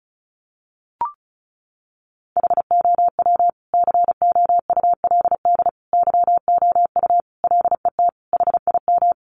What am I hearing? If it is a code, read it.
Morse code, heard as a keyed tone, about 34 wpm, 700 Hz. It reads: HOW COULD YOU LET HIM
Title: Sentences from Top 300 Words 35wpm